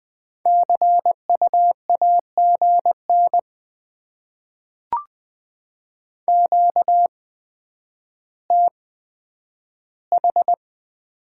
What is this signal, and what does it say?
Morse code, heard as a keyed tone, about 20 wpm, spaced as for 10 wpm, 700 Hz. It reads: CU AGN QTH
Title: Top 20 QSO Elements - 8x Character Spacing - 20wpm